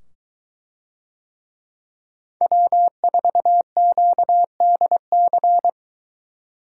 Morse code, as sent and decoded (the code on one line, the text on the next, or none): .-- ....- --.- -.. -.-.
W4QDC